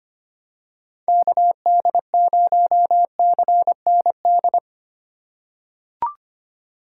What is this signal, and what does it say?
Morse code, heard as a keyed tone, about 25 wpm, 700 Hz. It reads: KD0CND
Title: USA Call Signs - 6 Characters - 3x Character Spacing - 25wpm